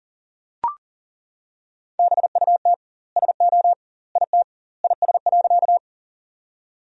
Morse code, heard as a keyed tone, about 40 wpm, 700 Hz. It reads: BUT SO IT IS.
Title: 2x Word-Spacing - Sentences from Top 100 Words - 40wpm